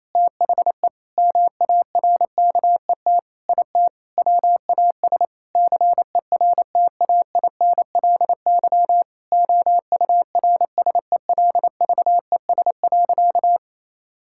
Morse code, heard as a keyed tone, about 28 wpm, 700 Hz. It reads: THE MARKET IT WAS CERTAINLY OURSELVES.
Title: Sentences from Top 1600 Words 28wpm